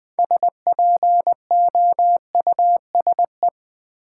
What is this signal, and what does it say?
Morse code, heard as a keyed tone, about 20 wpm, 700 Hz. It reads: SPOUSE
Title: Build A Word - 2 Letters At A Time - 6-Letter Words - 20wpm